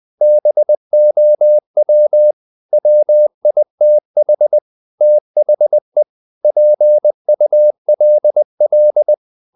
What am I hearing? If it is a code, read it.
Morse code, heard as a keyed tone, about 20 wpm, 600 Hz. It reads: BOW WITH THE PULL
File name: the old man and the sea part 2 20wpm